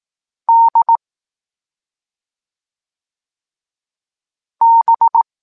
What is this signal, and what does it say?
Morse code, heard as a keyed tone, about 18 wpm, 920 Hz. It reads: D B
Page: Tekst 14 – doskonalenie odbioru poznanych liter